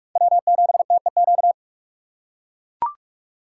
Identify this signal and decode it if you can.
Morse code, heard as a keyed tone, about 43 wpm, 700 Hz. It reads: W7TEQ